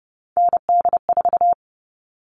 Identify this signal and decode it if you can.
Morse code, heard as a keyed tone, about 30 wpm, 700 Hz. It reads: ND4
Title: All Cut Numbers - 3-Digit 30wpm